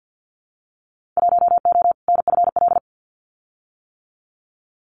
Morse code, heard as a keyed tone, about 48 wpm, 700 Hz. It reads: 1O NFL
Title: Winter Field Day Exchanges 50wpm